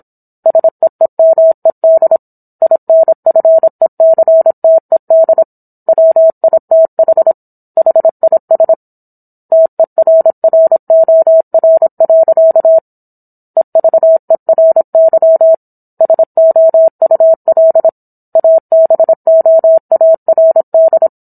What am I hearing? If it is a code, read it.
Morse code, heard as a keyed tone, about 27 wpm, 650 Hz. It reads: SEEMED INFECTED WITH HIS TERROR. EVERY SOUL ABOARD